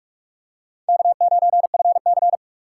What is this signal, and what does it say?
Morse code, heard as a keyed tone, about 44 wpm, 690 Hz. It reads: K9FC
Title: USA Call Signs - 4 Characters - 2x Character Spacing - 45wpm